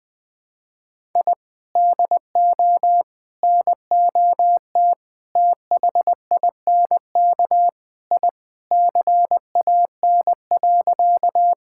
Morse code, heard as a keyed tone, about 20 wpm, 700 Hz. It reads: I DO NOT THINK I CAN.
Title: Sentences with 4-Word Phrases - Spoken Beginning - 20wpm